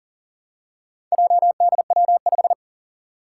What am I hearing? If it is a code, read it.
Morse code, heard as a keyed tone, about 41 wpm, 700 Hz. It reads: JDW5
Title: Four Challenging Letter-Number Combinations 40wpm